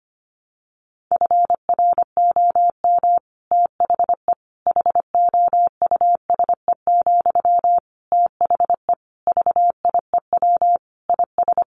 Morse code, heard as a keyed tone, about 25 wpm, 700 Hz. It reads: FROM THE HOUSE, THE VIEW IS
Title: Sentences from Top 1100 Words 25wpm